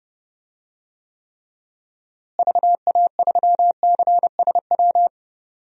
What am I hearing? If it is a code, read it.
Morse code, heard as a keyed tone, about 30 wpm, 700 Hz. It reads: VA3CSW
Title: DXCC-100 Call Signs - 6 Characters - Repeated 2x - 30wpm